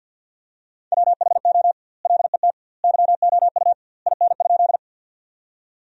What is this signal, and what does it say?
Morse code, heard as a keyed tone, about 50 wpm, 700 Hz. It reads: WHO LET YOU IN?
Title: 2x Word-Spacing - Sentences from Top 300 Words - 50wpm